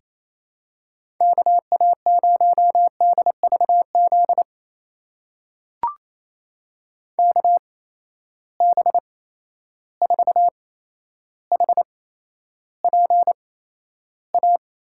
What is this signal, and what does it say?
Morse code, heard as a keyed tone, about 28 wpm, spaced as for 12 wpm, 700 Hz. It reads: KA0DVZ KB4HPA
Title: USA Call Signs - 6 Characters - 8x Character Spacing - 28wpm